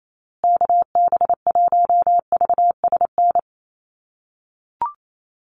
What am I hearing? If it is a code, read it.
Morse code, heard as a keyed tone, about 28 wpm, 700 Hz. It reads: KB1VSN
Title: USA Call Signs Repeated 2x 28wpm